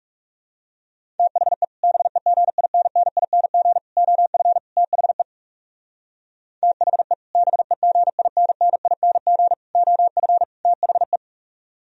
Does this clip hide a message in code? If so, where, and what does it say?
Yes: Morse code from 1.2 to 11.2 s, heard as a keyed tone, about 38 wpm, 690 Hz. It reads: THE BEGINNING OF THE THE BEGINNING OF THE